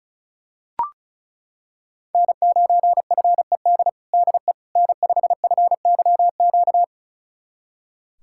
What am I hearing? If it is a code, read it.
Morse code, heard as a keyed tone, about 36 wpm, 700 Hz. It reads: N9FED DE N5FYQ
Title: USA Callsign DE Callsign - 5 Characters - 35wpm